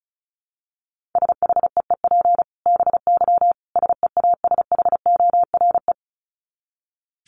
Morse code, heard as a keyed tone, about 34 wpm, 700 Hz. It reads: SHEEP BY SEASHORE
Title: Dit Challenge 35wpm